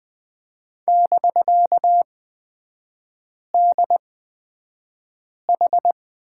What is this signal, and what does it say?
Morse code, heard as a keyed tone, about 20 wpm, spaced as for 5 wpm, 700 Hz. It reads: <BK>DH
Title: 250 Lesson 40e - BK - Three Characters 20wpm